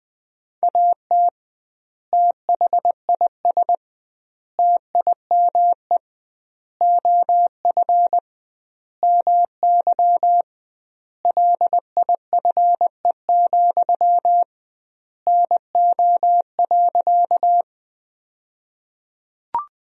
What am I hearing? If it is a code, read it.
Morse code, heard as a keyed tone, about 21 wpm, 700 Hz. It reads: AT THIS TIME OF MY LIFE, NO.
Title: 2x Word-Spacing - Sentences from Top 300 Words - 20wpm